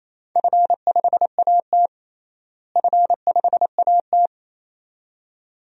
Morse code, heard as a keyed tone, about 28 wpm, 700 Hz. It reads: F5AT F5AT
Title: DXCC-100 Call Signs - 4 Characters - Repeated 2x - 28wpm